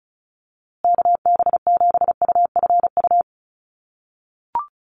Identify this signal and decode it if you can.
Morse code, heard as a keyed tone, about 34 wpm, 700 Hz. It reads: KB7UFU
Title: USA Call Signs Repeated 3x 35wpm